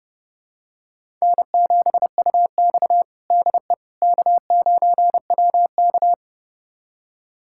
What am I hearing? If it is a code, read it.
Morse code, heard as a keyed tone, about 31 wpm, 700 Hz. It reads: N7UX DE K9WK
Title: USA Callsign DE Callsign - 4 Characters - 30wpm